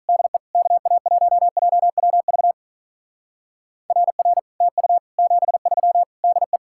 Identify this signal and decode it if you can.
Morse code, heard as a keyed tone, about 44 wpm, 700 Hz. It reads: DE KA1JWV RR TU 73 DE